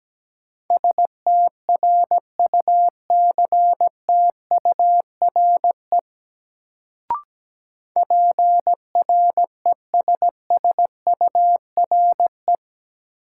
Morse code, heard as a keyed tone, about 17 wpm, 700 Hz. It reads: STRUCTURE PRESSURE
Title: Sentences from Top 1000 Words - Review Part 6 - New Word Drill - 17wpm